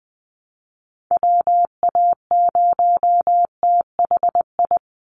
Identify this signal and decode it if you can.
Morse code, heard as a keyed tone, about 20 wpm, 700 Hz. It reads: WA0THI
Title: USA Call Signs 20wpm